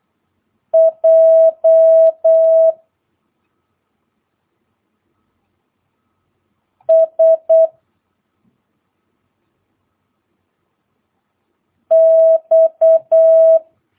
Morse code, heard as a keyed tone, about 8 wpm, 650 Hz. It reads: J S X
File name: jh8cbh / 2022cw